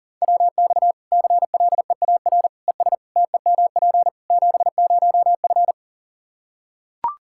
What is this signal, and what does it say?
Morse code, heard as a keyed tone, about 40 wpm, 700 Hz. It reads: WX CLEAR ES TEMP 70F
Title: Practice QSO - Exchange 2 - Info - 40wpm